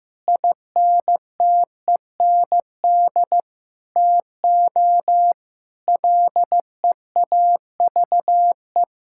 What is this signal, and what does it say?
Morse code, heard as a keyed tone, about 14 wpm, 700 Hz. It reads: INTEND TO LEAVE
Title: Sentences from Top 1600 Words 15wpm